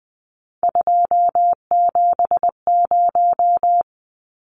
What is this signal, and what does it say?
Morse code, heard as a keyed tone, about 20 wpm, 700 Hz. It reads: 270
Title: Numbers - 3-Digit 20wpm